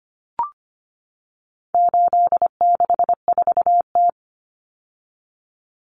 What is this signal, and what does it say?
Morse code, heard as a keyed tone, about 25 wpm, 700 Hz. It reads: 864T
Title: Common Cut Numbers - 4-Digit 25wpm